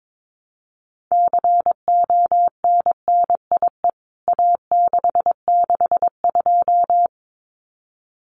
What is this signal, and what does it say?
Morse code, heard as a keyed tone, about 22 wpm, 700 Hz. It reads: CONNIE A662
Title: CWT Exchange 22wpm